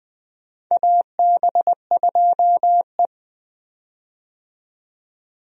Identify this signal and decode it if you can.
Morse code, heard as a keyed tone, about 21 wpm, 700 Hz. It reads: AB2E
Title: USA Call Signs - 4 Characters - 20wpm